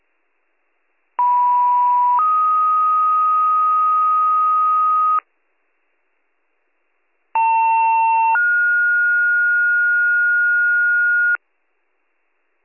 Two-tone page, heard as a one-second tone, then a three-second tone, about 980 then 1250 Hz, about 900 then 1430 Hz. It is heard from 1.2 to 11.4 s.